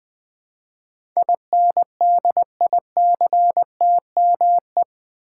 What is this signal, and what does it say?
Morse code, heard as a keyed tone, about 20 wpm, 700 Hz. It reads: INDICTME
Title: Build A Word - 4 Letters At A Time - 10-Letter Words - 20wpm